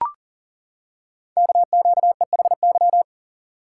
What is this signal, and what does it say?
Morse code, heard as a keyed tone, about 38 wpm, 700 Hz. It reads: KQEHY
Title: Code Groups - 5 Letters - 40wpm